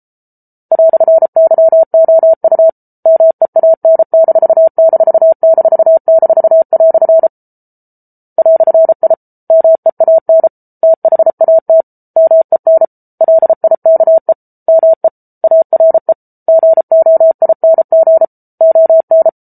Morse code, heard as a keyed tone, about 34 wpm, 650 Hz. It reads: "YOU MEAN----" "I MEAN THAT MEN LIKE ME ARE GOING ON